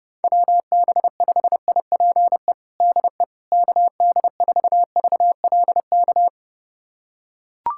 Morse code, heard as a keyed tone, about 30 wpm, 700 Hz. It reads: WB5IPE DE KD4VLK
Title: USA Callsign DE Callsign 30wpm